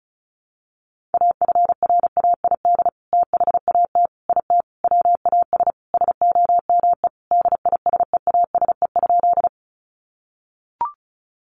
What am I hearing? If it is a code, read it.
Morse code, heard as a keyed tone, about 34 wpm, 700 Hz. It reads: AFRAID THAT IT WAS SOME DISEASE?
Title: Sentences from Top 1500 Words 35wpm